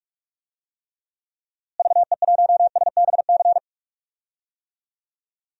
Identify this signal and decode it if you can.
Morse code, heard as a keyed tone, about 46 wpm, 700 Hz. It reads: VE1SBC
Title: DXCC-100 Call Signs - 6 Characters - Repeated 2x - 45wpm